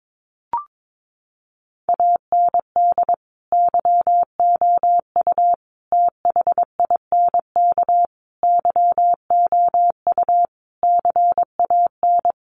Morse code, heard as a keyed tone, about 22 wpm, 700 Hz. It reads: AND YOU THINK YOU CAN
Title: Sentences from Top 1300 Words 22wpm